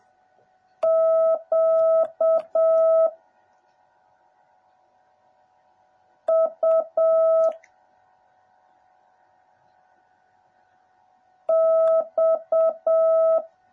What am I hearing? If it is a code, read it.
Morse code, heard as a keyed tone, about 7 wpm, 650 Hz. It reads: Q U X
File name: jh8cbh / 2021cw / cw09a